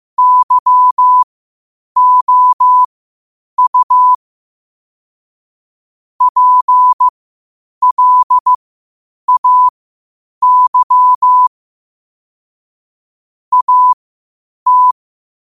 Morse code, heard as a keyed tone, about 15 wpm, spaced as for 8 wpm, 1000 Hz. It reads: YOU PLAY AT
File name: qotdfiles / 8wpm